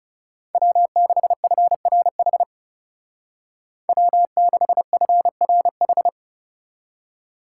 Morse code, heard as a keyed tone, about 37 wpm, 700 Hz. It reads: W6FRH W6FRH
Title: Speed Racing - USA 1x3 Call Signs 30wpm